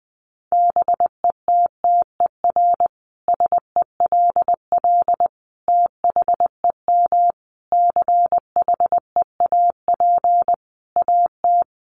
Morse code, heard as a keyed tone, about 20 wpm, 700 Hz. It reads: BETTER SELL THEM CHEAP AT
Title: Sentences from Top 1200 Words 20wpm